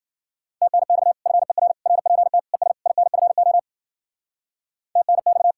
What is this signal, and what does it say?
Morse code, heard as a keyed tone, about 52 wpm, 700 Hz. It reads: TNX FER RPT ES INFO TNX